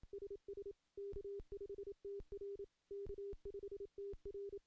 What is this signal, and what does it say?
Morse code, heard as a keyed tone, about 27 wpm, 400 Hz. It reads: SS K5TR K5TR
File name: contests / CW-SS-k5tr- -1300